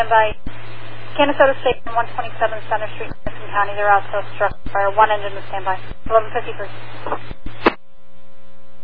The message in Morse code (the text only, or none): none